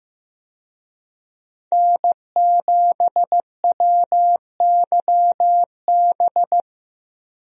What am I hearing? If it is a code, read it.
Morse code, heard as a keyed tone, about 15 wpm, 700 Hz. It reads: N7WYB
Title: USA Call Signs - 5 Characters - Repeated 3x - 15wpm